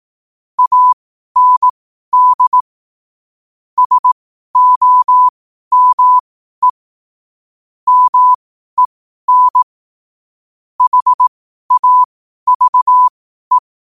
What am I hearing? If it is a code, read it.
Morse code, heard as a keyed tone, about 18 wpm, spaced as for 12 wpm, 1000 Hz. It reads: AND SOME MEN HAVE